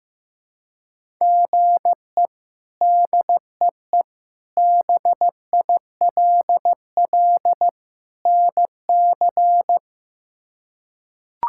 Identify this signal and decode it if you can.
Morse code, heard as a keyed tone, about 15 wpm, 700 Hz. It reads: GE DEE BILL NC
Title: K1USN SST QSO Practice - 15wpm